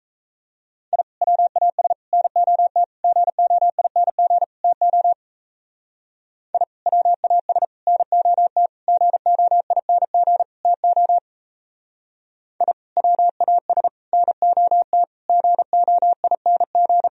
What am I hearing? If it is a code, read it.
Morse code, heard as a keyed tone, about 32 wpm, 700 Hz. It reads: I WAS NOT GOING TO I WAS NOT GOING TO I WAS NOT GOING
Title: Speed-Racing - Sets of 5 Words 28wpm